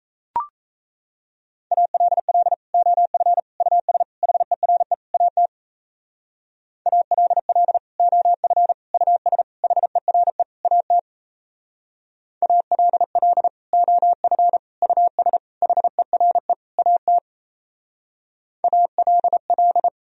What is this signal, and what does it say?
Morse code, heard as a keyed tone, about 38 wpm, 700 Hz. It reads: ALL OF US HERE AT ALL OF US HERE AT ALL OF US HERE AT ALL